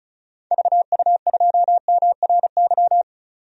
Morse code, heard as a keyed tone, about 34 wpm, 700 Hz. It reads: VU2MRY
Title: DXCC-100 Call Signs - 6 Characters - 35wpm